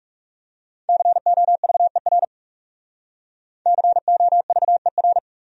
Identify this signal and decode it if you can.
Morse code, heard as a keyed tone, about 38 wpm, 700 Hz. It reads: COVER COVER